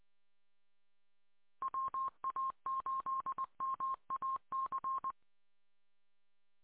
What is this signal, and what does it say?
Morse code, heard as a keyed tone, about 20 wpm, 1060 Hz. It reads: WA8MAC